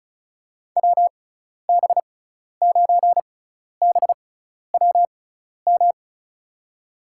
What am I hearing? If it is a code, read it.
Morse code, heard as a keyed tone, about 36 wpm, spaced as for 12 wpm, 700 Hz. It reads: WB9BWM
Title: USA Call Signs - 6 Characters - 6x Character Spacing - 35wpm